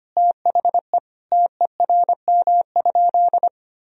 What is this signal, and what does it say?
Morse code, heard as a keyed tone, about 25 wpm, 700 Hz. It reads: THE TERM?
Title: Sentences from Top 1000 Words - Review Part 4 - 25wpm